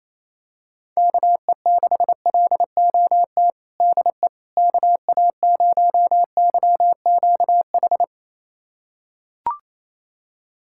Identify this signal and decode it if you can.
Morse code, heard as a keyed tone, about 28 wpm, 700 Hz. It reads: KE6LOT DE KA0YQH
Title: USA Callsign DE Callsign 28wpm